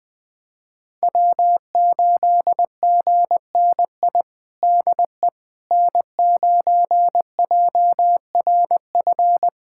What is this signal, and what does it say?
Morse code, heard as a keyed tone, about 21 wpm, 700 Hz. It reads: W8GNI DE N9JRF
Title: USA Callsign DE Callsign - 5 Characters - 20wpm